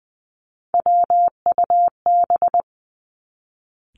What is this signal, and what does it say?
Morse code, heard as a keyed tone, about 20 wpm, 700 Hz. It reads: WUB
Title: Three Challenging Letter-Number Combinations 20wpm